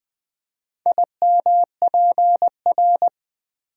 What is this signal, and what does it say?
Morse code, heard as a keyed tone, about 20 wpm, 700 Hz. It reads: IMPR
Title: Build A Word - 4 Letters At A Time - 10-Letter Words - 20wpm